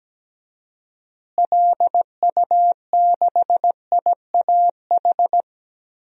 Morse code, heard as a keyed tone, about 17 wpm, 700 Hz. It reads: LU6IAH
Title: DXCC-100 Call Signs - 6 Characters - Repeated 2x - 17wpm